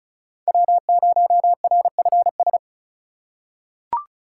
Morse code, heard as a keyed tone, about 34 wpm, 700 Hz. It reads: W0RFS
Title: USA Call Signs - 5 Characters - 4x Character Spacing - 35wpm